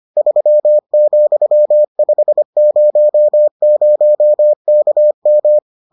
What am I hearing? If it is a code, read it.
Morse code, heard as a keyed tone, about 25 wpm, 600 Hz. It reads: 3,500KM